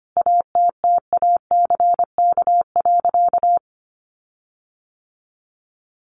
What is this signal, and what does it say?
Morse code, heard as a keyed tone, about 25 wpm, 700 Hz. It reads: ATTACK.
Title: Sentences from Top 1500 Words 25wpm